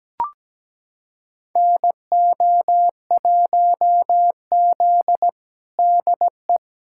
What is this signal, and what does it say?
Morse code, heard as a keyed tone, about 17 wpm, 700 Hz. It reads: NO1Z DE
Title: USA Callsign DE Callsign - 4 Characters - 17wpm